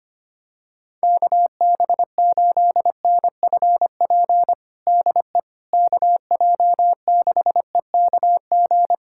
Morse code, heard as a keyed tone, about 25 wpm, 700 Hz. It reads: KB8NFP DE KJ6EKG
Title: USA Callsign DE Callsign - 6 Characters - 25wpm